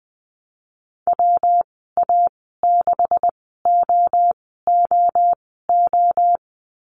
Morse code, heard as a keyed tone, about 20 wpm, spaced as for 15 wpm, 700 Hz. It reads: WA6OOO
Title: 173 Lesson 28g - Number 9 - Call Signs 20wpm